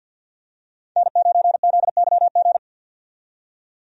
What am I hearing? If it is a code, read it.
Morse code, heard as a keyed tone, about 48 wpm, 700 Hz. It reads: N9ZYG
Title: USA Call Signs - 5 Characters - 50wpm